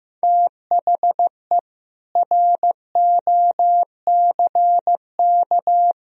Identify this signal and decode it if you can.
Morse code, heard as a keyed tone, about 15 wpm, 700 Hz. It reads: THE ROCK
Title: Sentences from Top 400 Words and Abbreviations 15wpm